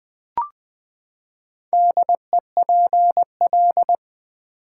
Morse code, heard as a keyed tone, about 20 wpm, 700 Hz. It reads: DEPL
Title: Build A Word - 4 Letters At A Time - 10-Letter Words - 20wpm